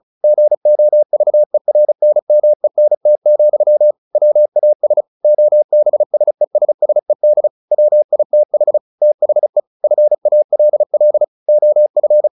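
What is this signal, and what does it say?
Morse code, heard as a keyed tone, about 34 wpm, 600 Hz. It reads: GOVERNMENT, WAS OBSESSED WITH THE FALL OF
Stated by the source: Morse practice at 35 wpm from CW med Gnister, sending Ken Follett's novel Never in English.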